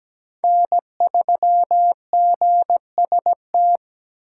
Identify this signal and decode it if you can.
Morse code, heard as a keyed tone, about 17 wpm, 700 Hz. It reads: N3GST
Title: USA Call Signs - 5 Characters - 4x Character Spacing - 17wpm